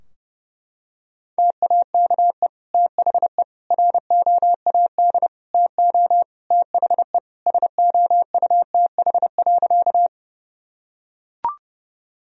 Morse code, heard as a keyed tone, about 30 wpm, 700 Hz. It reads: TAKE THE ROAD TO THE SOUTH.
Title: Sentences from Top 1000 Words - Review Part 2 - 30wpm